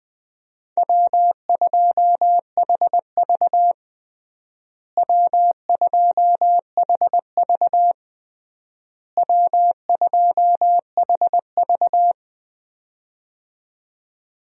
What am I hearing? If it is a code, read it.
Morse code, heard as a keyed tone, about 20 wpm, 700 Hz. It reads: W2HV W2HV W2HV
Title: USA Call Signs - 4 Characters - Repeated 3x - 20wpm